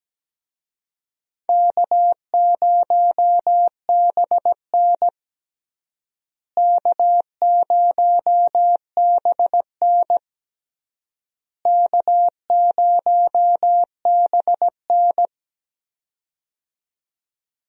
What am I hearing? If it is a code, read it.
Morse code, heard as a keyed tone, about 17 wpm, 700 Hz. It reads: K0BN K0BN K0BN
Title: USA Call Signs - 4 Characters - Repeated 3x - 17wpm